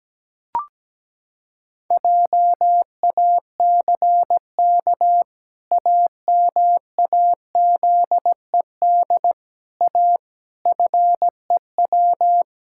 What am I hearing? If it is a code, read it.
Morse code, heard as a keyed tone, about 17 wpm, 700 Hz. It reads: JACK AMAZED A FEW
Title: Pangram Challenge v2 - 17wpm